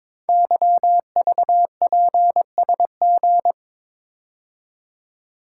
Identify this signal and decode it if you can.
Morse code, heard as a keyed tone, about 22 wpm, 700 Hz. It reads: YVPSG
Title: Code Groups - 5 Letters - 22wpm